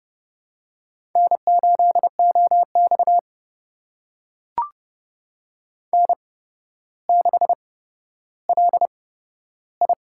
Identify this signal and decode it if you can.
Morse code, heard as a keyed tone, about 32 wpm, spaced as for 13 wpm, 700 Hz. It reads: N8OX N6LI
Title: USA Call Signs - 4 Characters - 8x Character Spacing - 30wpm